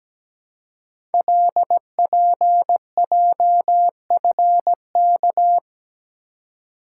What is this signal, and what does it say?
Morse code, heard as a keyed tone, about 17 wpm, 700 Hz. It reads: LPJFK
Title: Code Groups - 5 Letters - 17wpm